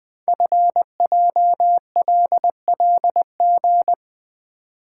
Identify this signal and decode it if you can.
Morse code, heard as a keyed tone, about 21 wpm, 700 Hz. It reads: FJLLG
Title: Code Groups - 5 Letters - 20wpm